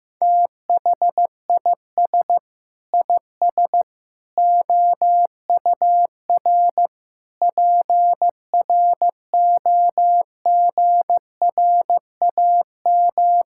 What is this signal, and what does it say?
Morse code, heard as a keyed tone, about 15 wpm, 700 Hz. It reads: THIS IS OUR PROGRAM
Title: Sentences from Top 1000 Words - Review Part 5 - 15wpm